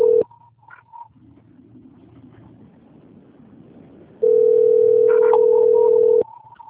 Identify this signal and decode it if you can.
Ringback tone (North America), 440 + 480 Hz.
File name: rg-1600-unknown- - - .18906